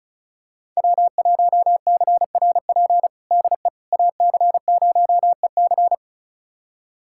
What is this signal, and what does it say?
Morse code, heard as a keyed tone, about 36 wpm, 690 Hz. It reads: W1CRP DE AC0EC
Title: USA Callsign DE Callsign - 5 Characters - 35wpm